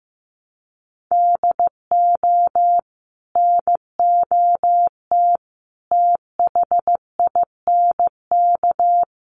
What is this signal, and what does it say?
Morse code, heard as a keyed tone, about 15 wpm, 700 Hz. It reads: DO NOT THINK
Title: Sets of 4 Words 15wpm